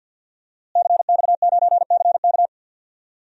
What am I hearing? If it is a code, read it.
Morse code, heard as a keyed tone, about 50 wpm, 700 Hz. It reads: CX9CK
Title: DXCC-100 Call Signs - 5 Characters - Repeated 2x - 50wpm